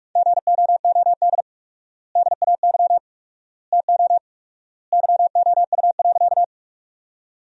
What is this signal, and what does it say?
Morse code, heard as a keyed tone, about 41 wpm, 700 Hz. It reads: GOOD DAY TO YOU.
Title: 4x Word-Spacing - Sentences from Top 200 Words - 45wpm